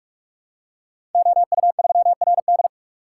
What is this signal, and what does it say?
Morse code, heard as a keyed tone, about 44 wpm, 700 Hz. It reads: OU3RD
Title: Code Groups - 5 Letter:Numbers - 45wpm